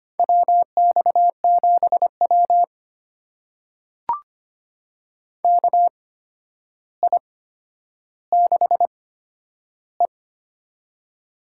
Morse code, heard as a keyed tone, about 25 wpm, spaced as for 11 wpm, 700 Hz. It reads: WX7W KI6E
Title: USA Call Signs - 4 Characters - 8x Character Spacing - 25wpm